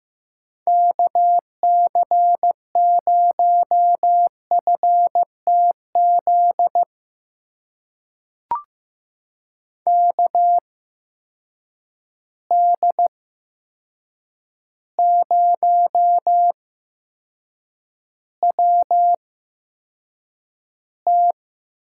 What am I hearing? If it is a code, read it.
Morse code, heard as a keyed tone, about 15 wpm, spaced as for 7 wpm, 700 Hz. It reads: KC0FTZ KD0WT
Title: USA Call Signs - 6 Characters - 8x Character Spacing - 15wpm